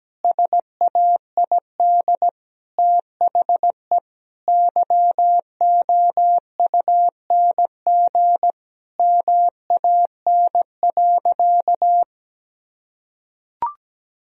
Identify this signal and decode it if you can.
Morse code, heard as a keyed tone, about 17 wpm, 700 Hz. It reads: SAID THE YOUNG MAN.
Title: Sentences from Top 1000 Words - Review Part 2 - 17wpm